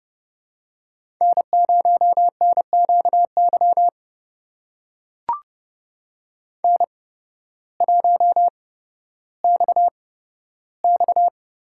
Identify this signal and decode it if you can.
Morse code, heard as a keyed tone, about 30 wpm, spaced as for 14 wpm, 700 Hz. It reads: N0NQY N1XX
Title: USA Call Signs - 5 Characters - 8x Character Spacing - 30wpm